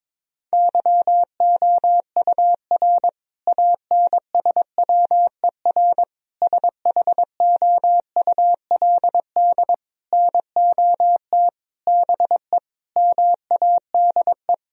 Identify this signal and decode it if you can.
Morse code, heard as a keyed tone, about 22 wpm, 700 Hz. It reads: YOUR ANSWER SHOULD NOT BE MADE
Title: Sentences from Top 1000 Words - Review Part 3 - 22wpm